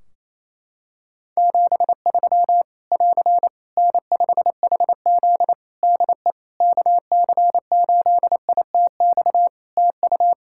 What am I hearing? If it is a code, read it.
Morse code, heard as a keyed tone, about 28 wpm, 700 Hz. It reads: 73 <AR> N5HZ DE KC8ITX TU
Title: Practice QSO - Exchange 4 - Ending - 28wpm